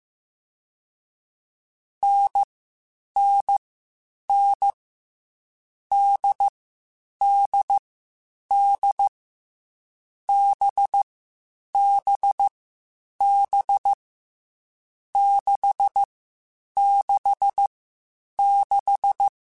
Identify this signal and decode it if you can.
Morse code, heard as a keyed tone, about 15 wpm, spaced as for 8 wpm, 800 Hz. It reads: NNN DDD BBB 666
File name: Difficult Combos / Difficult Combos-mp3s / NDB NDB6 @13